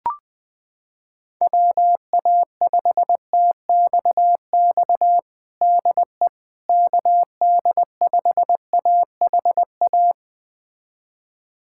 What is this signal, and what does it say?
Morse code, heard as a keyed tone, about 20 wpm, 700 Hz. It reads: WA5TXX DE KD5AHA
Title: USA Callsign DE Callsign - 6 Characters - 20wpm